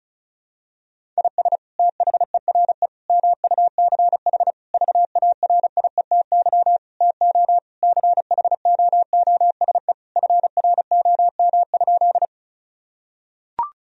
Morse code, heard as a keyed tone, about 34 wpm, 700 Hz. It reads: IS THERE MUCH VARIETY TO CHOOSE FROM?
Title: Sentences from Top 1000 Words - Review Part 4 - 35wpm